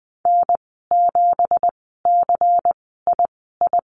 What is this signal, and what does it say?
Morse code, heard as a keyed tone, about 20 wpm, spaced as for 15 wpm, 700 Hz. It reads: N7CII
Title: 179 Lesson 29g - Slash - Call Signs 20wpm